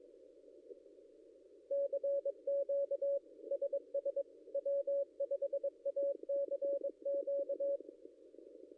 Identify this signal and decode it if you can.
Morse code, heard as a keyed tone, about 22 wpm, 560 Hz. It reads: CQ SS W5ACQ
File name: contests / ss-cw-k5tr-op-n5rz -0800